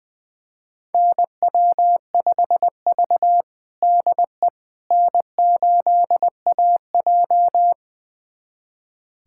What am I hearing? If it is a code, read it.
Morse code, heard as a keyed tone, about 20 wpm, 700 Hz. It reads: NW5V DE N8AJ
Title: USA Callsign DE Callsign - 4 Characters - 20wpm